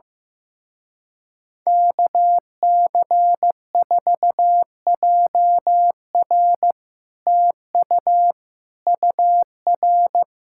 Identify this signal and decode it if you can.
Morse code, heard as a keyed tone, about 15 wpm, 700 Hz. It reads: KC4JR TU UR
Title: Speed-Racing Common QSO Phrases Encore - 15wpm